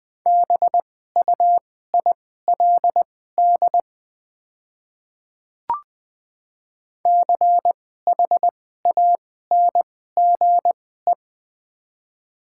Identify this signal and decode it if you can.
Morse code, heard as a keyed tone, about 20 wpm, spaced as for 14 wpm, 700 Hz. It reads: BUILD CHANGE
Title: 2x Character Spacing - Top 200 Words - 20wpm